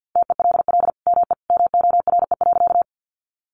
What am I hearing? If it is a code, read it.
Morse code, heard as a keyed tone, about 50 wpm, 700 Hz. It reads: TELL ME MORE.
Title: Sentences from Top 300 Words 50wpm